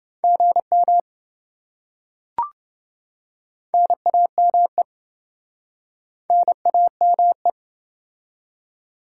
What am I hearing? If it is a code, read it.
Morse code, heard as a keyed tone, about 28 wpm, 700 Hz. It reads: GM NAME NAME